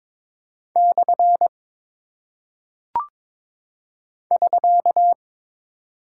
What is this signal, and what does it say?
Morse code, heard as a keyed tone, about 22 wpm, 700 Hz. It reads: / <SK>
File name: practice / 22wpm / Single Punctuation-Prosign - 22wpm